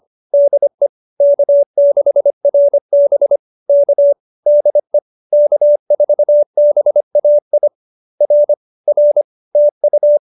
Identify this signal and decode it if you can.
Morse code, heard as a keyed tone, about 25 wpm, 590 Hz. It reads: DE K6RB K DE K4BAI R R TU
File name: QSO301 25